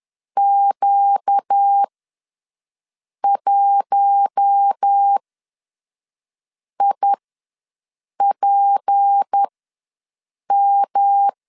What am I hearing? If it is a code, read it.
Morse code, heard as a keyed tone, about 10 wpm, spaced as for 5 wpm, 800 Hz. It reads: Q1IPM
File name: LEZIONI / 6 PKX0 / 6 P ws 10wpm 3751